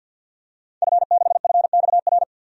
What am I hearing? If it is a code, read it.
Morse code, heard as a keyed tone, about 47 wpm, 700 Hz. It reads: F6FXR
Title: DXCC-100 Call Signs - 5 Characters - Repeated 2x - 50wpm